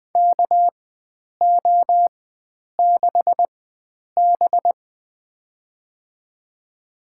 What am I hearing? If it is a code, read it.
Morse code, heard as a keyed tone, about 20 wpm, spaced as for 9 wpm, 700 Hz. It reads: KO6B
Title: USA Call Signs - 4 Characters - 4x Character Spacing - 20wpm